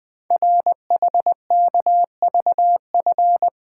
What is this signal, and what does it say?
Morse code, heard as a keyed tone, about 20 wpm, 700 Hz. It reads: RHKVF
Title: Code Groups - 5 Letters - 20wpm